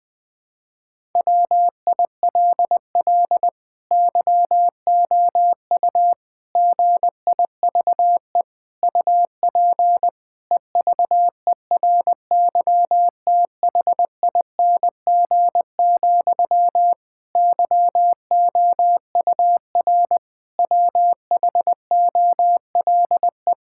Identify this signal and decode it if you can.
Morse code, heard as a keyed tone, about 21 wpm, 700 Hz. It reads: WILL YOU GIVE UP EVERYTHING, YOUR WHOLE
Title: Sentences from Top 1000 Words - Review Part 5 - 20wpm